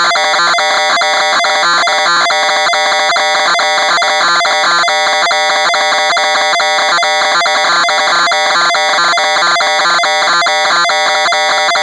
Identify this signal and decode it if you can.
Two-tone teleprinter signal, about 420 Hz shift.